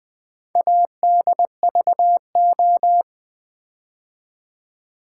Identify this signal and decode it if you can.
Morse code, heard as a keyed tone, about 20 wpm, 700 Hz. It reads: ADVO
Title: Build A Word - 4 Letters At A Time - 8-Letter Words - 20wpm